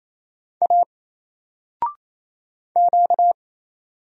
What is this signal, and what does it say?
Morse code, heard as a keyed tone, about 28 wpm, 700 Hz. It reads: A Q
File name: Single-Letter-Number-28wpm